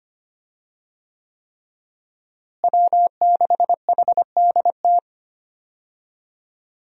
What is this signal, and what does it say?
Morse code, heard as a keyed tone, about 25 wpm, 700 Hz. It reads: W6HDT
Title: USA Call Signs - 5 Characters - Repeated 3x - 25wpm